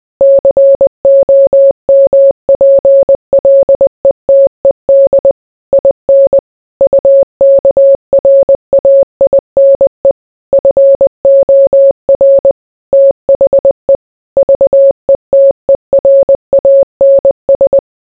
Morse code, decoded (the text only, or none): COMPLETED IN UKRAINE FOR THE VETERANS